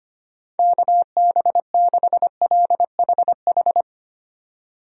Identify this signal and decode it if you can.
Morse code, heard as a keyed tone, about 25 wpm, 700 Hz. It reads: KB6LHH
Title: USA Call Signs - 6 Characters - 25wpm